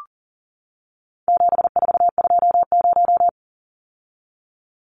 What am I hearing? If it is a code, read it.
Morse code, heard as a keyed tone, about 40 wpm, 700 Hz. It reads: 7420
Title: Numbers - 4-Digit 40wpm